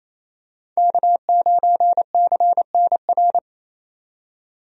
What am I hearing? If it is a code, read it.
Morse code, heard as a keyed tone, about 28 wpm, 700 Hz. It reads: K9CNR
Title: USA Call Signs - 5 Characters - 28wpm